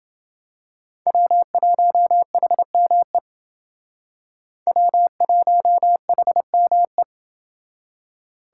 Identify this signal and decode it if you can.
Morse code, heard as a keyed tone, about 29 wpm, 700 Hz. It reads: W1HME W1HME